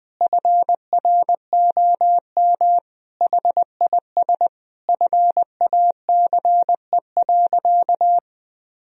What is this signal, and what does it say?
Morse code, heard as a keyed tone, about 19 wpm, 700 Hz. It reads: FROM HIS FACE.
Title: Sentences from Top 1000 Words - Review Part 2 - 20wpm